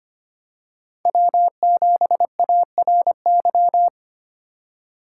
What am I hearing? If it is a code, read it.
Morse code, heard as a keyed tone, about 25 wpm, 700 Hz. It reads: W7ARY
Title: USA Call Signs - 6x Character Spacing - 25wpm